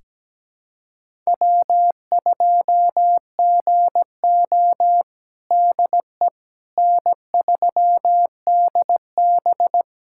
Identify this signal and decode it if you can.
Morse code, heard as a keyed tone, about 17 wpm, 700 Hz. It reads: W2GO DE N3DB
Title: USA Callsign DE Callsign - 4 Characters - 17wpm